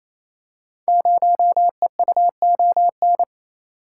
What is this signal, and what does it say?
Morse code, heard as a keyed tone, about 29 wpm, 700 Hz. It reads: 0EUON
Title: Code Groups - 5 Letter:Numbers - 28wpm